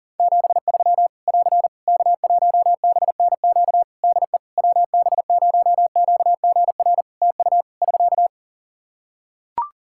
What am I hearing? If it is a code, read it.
Morse code, heard as a keyed tone, about 40 wpm, 700 Hz. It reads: 73 <AR> K1BNQ DE WB0QGR TU <SK>
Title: Practice QSO - Exchange 4 - Ending - 40wpm